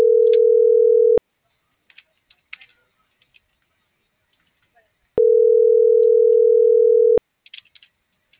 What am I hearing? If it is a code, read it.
Ringback tone (North America), 440 + 480 Hz.